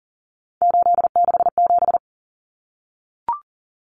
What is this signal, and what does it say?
Morse code, heard as a keyed tone, about 40 wpm, 700 Hz. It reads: 867
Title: Common Cut Numbers - 3-Digit 40wpm